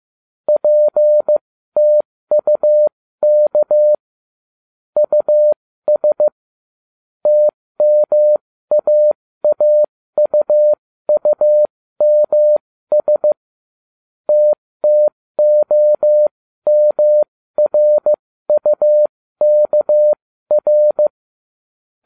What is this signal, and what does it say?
Morse code, heard as a keyed tone, about 15 wpm, spaced as for 13 wpm, 600 Hz. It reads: PTUK US TMAAUUMS TTOMRUKR